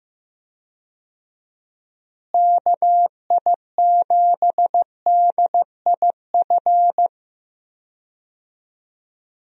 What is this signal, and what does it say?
Morse code, heard as a keyed tone, about 15 wpm, 700 Hz. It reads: KI7DIF
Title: USA Call Signs - 6 Characters - Repeated 3x - 15wpm